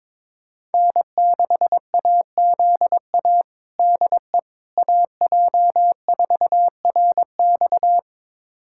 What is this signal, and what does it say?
Morse code, heard as a keyed tone, about 21 wpm, 700 Hz. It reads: N6AZA DE AJ4RX
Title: USA Callsign DE Callsign - 5 Characters - 22wpm